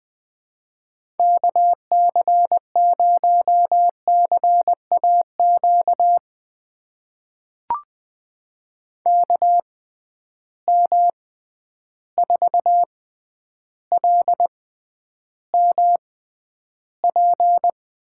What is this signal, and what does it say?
Morse code, heard as a keyed tone, about 20 wpm, spaced as for 10 wpm, 700 Hz. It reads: KC0CAQ KM4LMP
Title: USA Call Signs - 6 Characters - 6x Character Spacing - 20wpm